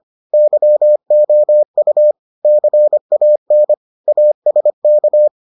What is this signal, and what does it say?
Morse code, heard as a keyed tone, about 25 wpm, 600 Hz. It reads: YOU CAN ASK